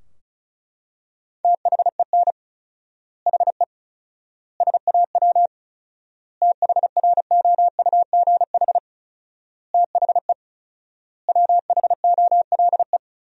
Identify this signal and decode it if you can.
Morse code, heard as a keyed tone, about 34 wpm, 700 Hz. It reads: THEN HE SAW THROUGH THE WHOLE
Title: 4x Word-Spacing - Sentences from Top 400 Words - 35wpm